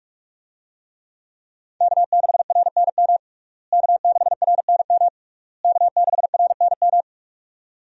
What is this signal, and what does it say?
Morse code, heard as a keyed tone, about 46 wpm, 700 Hz. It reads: K6RNM K6RNM K6RNM
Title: USA Call Signs - 5 Characters - Repeated 3x - 45wpm